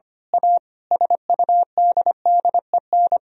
Morse code, heard as a keyed tone, about 25 wpm, 700 Hz. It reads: A SUDDEN
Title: Speed-Racing - Sets of 5 Words 25wpm